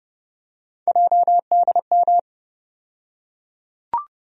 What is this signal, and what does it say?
Morse code, heard as a keyed tone, about 32 wpm, 700 Hz. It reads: JDM
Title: USA Call Sign Postfixes - 3 Letters 30wpm